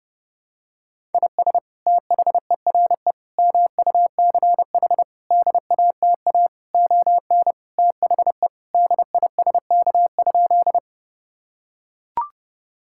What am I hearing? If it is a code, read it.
Morse code, heard as a keyed tone, about 32 wpm, 700 Hz. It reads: IS THERE MUCH DATA ON THE DISK?
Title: Sentences from Top 1000 Words - Review Part 6 - 30wpm